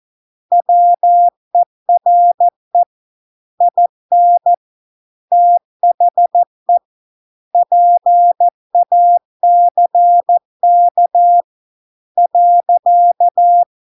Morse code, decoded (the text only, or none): WERE IN THE PACK .